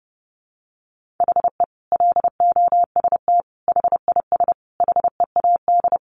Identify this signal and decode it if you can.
Morse code, heard as a keyed tone, about 30 wpm, 700 Hz. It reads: HE LOST HIS HEAD
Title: Sentences from Top 1400 Words 30wpm